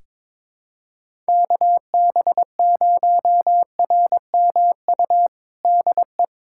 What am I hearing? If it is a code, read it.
Morse code, heard as a keyed tone, about 22 wpm, 700 Hz. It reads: KB0RMU DE
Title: USA Callsign DE Callsign - 6 Characters - 22wpm